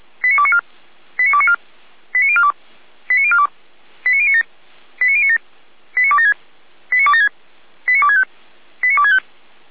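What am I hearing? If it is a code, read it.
Five-tone selective call (ZVEI), heard as quick runs of five tones, tones of about 70 ms. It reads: 89294 89294 89042 89042 89097 89097 89276 89276 89265 89265
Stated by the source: ZVEI five-tone selective call received by SDR, feuerwehr-orlamuende.org fire brigade.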